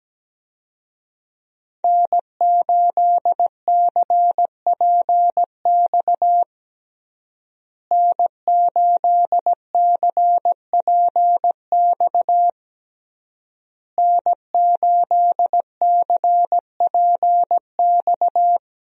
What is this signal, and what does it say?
Morse code, heard as a keyed tone, about 17 wpm, 700 Hz. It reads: N8CPX N8CPX N8CPX
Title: USA Call Signs - 5 Characters - Repeated 3x - 17wpm